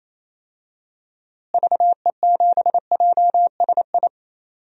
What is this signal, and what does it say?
Morse code, heard as a keyed tone, about 28 wpm, 700 Hz. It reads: VE7JSI
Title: DXCC-100 Call Signs - 6 Characters - Repeated 3x - 28wpm